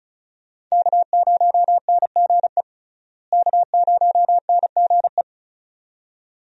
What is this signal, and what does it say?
Morse code, heard as a keyed tone, about 36 wpm, 700 Hz. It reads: K0NGE K0NGE
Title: USA Call Signs - 5 Characters - Repeated 2x - 35wpm